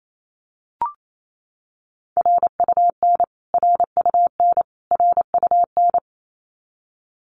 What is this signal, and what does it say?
Morse code, heard as a keyed tone, about 28 wpm, 700 Hz. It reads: RUN RUN RUN
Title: Top 300 Words Repeated 28wpm